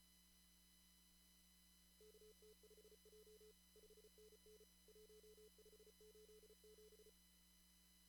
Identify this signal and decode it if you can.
Morse code, heard as a keyed tone, about 34 wpm, 440 Hz. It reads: KT5J 5NN 1587
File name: CW / kt5j wpx cw-op-k5pi